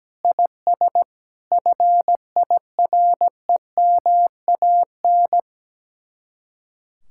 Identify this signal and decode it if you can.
Morse code, heard as a keyed tone, about 17 wpm, 700 Hz. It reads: IS FIREMAN
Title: Speed-Racing Common QSO Phrases - 17wpm